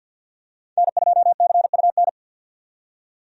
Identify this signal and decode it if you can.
Morse code, heard as a keyed tone, about 48 wpm, 700 Hz. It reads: N2CUN
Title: USA Call Signs - 5 Characters - 50wpm